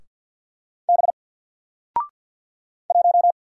Morse code, heard as a keyed tone, about 50 wpm, 700 Hz. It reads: B 1
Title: Single Letter-Number 50wpm